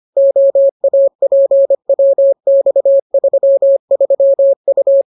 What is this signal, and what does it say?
Morse code, heard as a keyed tone, about 25 wpm, 550 Hz. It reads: OAPWX33U